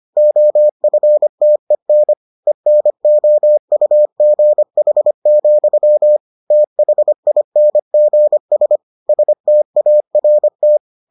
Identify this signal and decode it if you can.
Morse code, heard as a keyed tone, about 25 wpm, 600 Hz. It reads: OFTEN ENOUGH, THINGS START